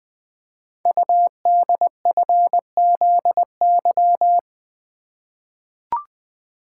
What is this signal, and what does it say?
Morse code, heard as a keyed tone, about 20 wpm, 700 Hz. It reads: UDFZY
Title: Code Groups - 5 Letters - 20wpm